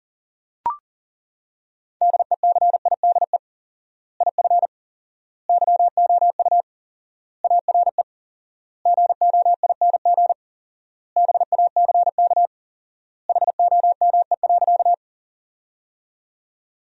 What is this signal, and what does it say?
Morse code, heard as a keyed tone, about 40 wpm, 700 Hz. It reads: DECIDE IF YOU ARE GOING BACK HOME.
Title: 4x Word-Spacing - Sentences from Top 500 Words - 40wpm